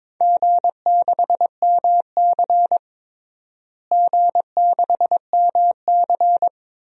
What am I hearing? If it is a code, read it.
Morse code, heard as a keyed tone, about 22 wpm, 700 Hz. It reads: G6MC G6MC
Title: DXCC-100 Call Signs - 4 Characters - Repeated 3x - 22wpm